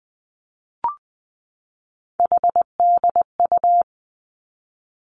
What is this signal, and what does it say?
Morse code, heard as a keyed tone, about 20 wpm, 700 Hz. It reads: HDU
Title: Three Challenging Letter-Number Combinations 20wpm